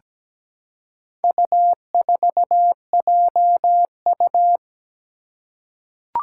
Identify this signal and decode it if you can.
Morse code, heard as a keyed tone, about 17 wpm, 700 Hz. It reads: U4JU
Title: Four Challenging Letter-Number Combinations 17wpm